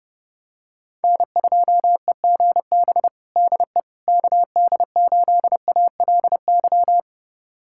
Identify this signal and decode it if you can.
Morse code, heard as a keyed tone, about 30 wpm, 700 Hz. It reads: N2EGB DE KD8ALY
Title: USA Callsign DE Callsign 30wpm